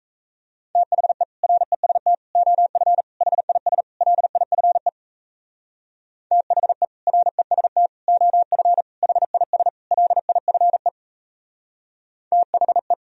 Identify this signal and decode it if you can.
Morse code, heard as a keyed tone, about 36 wpm, 700 Hz. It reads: THE REST OF HIS LIFE THE REST OF HIS LIFE THE